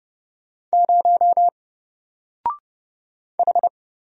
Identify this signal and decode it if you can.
Morse code, heard as a keyed tone, about 30 wpm, 700 Hz. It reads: 0 H
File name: Single-Letter-Number-30wpm